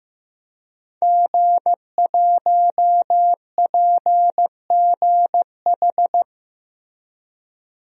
Morse code, heard as a keyed tone, about 15 wpm, 700 Hz. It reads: G1PGH
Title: DXCC-100 Call Signs - 5 Characters - Repeated 2x - 15wpm